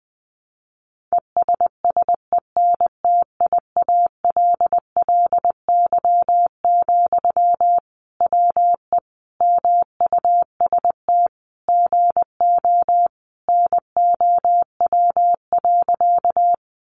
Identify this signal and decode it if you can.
Morse code, heard as a keyed tone, about 20 wpm, 700 Hz. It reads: ESSENTIALLY, WE MUST GO NOW.
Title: Sentences from Top 1300 Words 20wpm